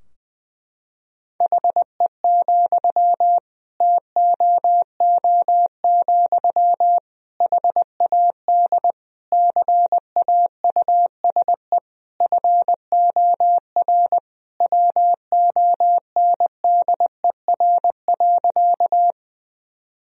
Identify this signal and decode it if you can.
Morse code, heard as a keyed tone, about 21 wpm, 700 Hz. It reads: HE, TOO, HAD CAUSE FOR WONDER.
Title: Sentences from Top 1000 Words - Review Part 3 - 20wpm